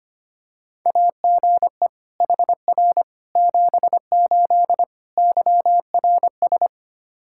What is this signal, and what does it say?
Morse code, heard as a keyed tone, about 25 wpm, 700 Hz. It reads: AGE HR 78 YRS
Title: Practice QSO - Exchange 3 - More Info - 25wpm